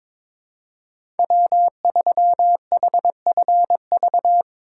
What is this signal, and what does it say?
Morse code, heard as a keyed tone, about 22 wpm, 700 Hz. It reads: W 3 H F V
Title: USA Call Signs - 5 Characters - 8x Character Spacing - 22wpm